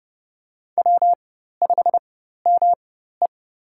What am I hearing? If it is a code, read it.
Morse code, heard as a keyed tone, about 31 wpm, spaced as for 14 wpm, 700 Hz. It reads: W5ME
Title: USA Call Signs - 4 Characters - 4x Character Spacing - 30wpm